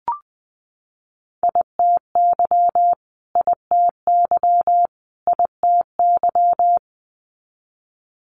Morse code, heard as a keyed tone, about 20 wpm, 700 Hz. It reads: ITY ITY ITY
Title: Multi-letter Phonemes Repeated 3x 20wpm